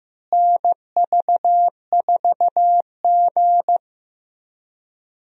Morse code, heard as a keyed tone, about 15 wpm, 700 Hz. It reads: NV4G
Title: USA Call Signs - 4 Characters - Repeated 3x - 15wpm